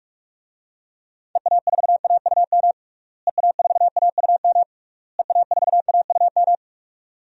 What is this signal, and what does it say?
Morse code, heard as a keyed tone, about 42 wpm, 700 Hz. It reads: EA4AUM EA4AUM EA4AUM
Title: DXCC-100 Call Signs - 6 Characters - Repeated 3x - 45wpm